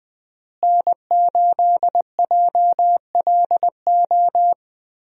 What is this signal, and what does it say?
Morse code, heard as a keyed tone, about 20 wpm, 700 Hz. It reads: N8JLO
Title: USA Call Signs - 5 Characters - Repeated 3x - 20wpm